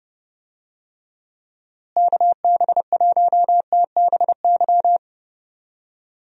Morse code, heard as a keyed tone, about 32 wpm, 700 Hz. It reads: KB1TBY
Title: USA Call Signs - 6 Characters - Repeated 3x - 30wpm